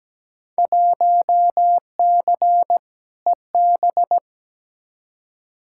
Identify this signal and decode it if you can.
Morse code, heard as a keyed tone, about 17 wpm, 700 Hz. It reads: 1C EB
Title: Field Day Exchanges Encore - 17wpm